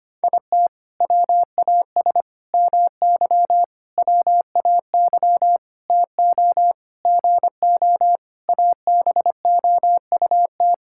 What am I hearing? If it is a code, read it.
Morse code, heard as a keyed tone, about 25 wpm, 700 Hz. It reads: IT WAS MY WAY TO GO ABOUT
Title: Sentences from Top 1600 Words 25wpm